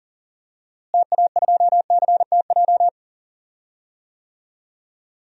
Morse code, heard as a keyed tone, about 40 wpm, 700 Hz. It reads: TA2CTJ
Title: DXCC-100 Call Signs - 6 Characters - Repeated 2x - 40wpm